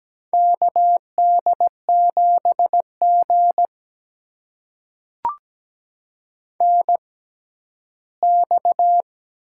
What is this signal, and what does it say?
Morse code, heard as a keyed tone, about 17 wpm, spaced as for 11 wpm, 700 Hz. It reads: KD7G NX
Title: USA Call Signs - 4 Characters - 6x Character Spacing - 17wpm